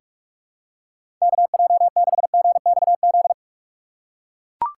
Morse code, heard as a keyed tone, about 43 wpm, 700 Hz. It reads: KJ6GXZ
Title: USA Call Signs - 3x Character Spacing - 45wpm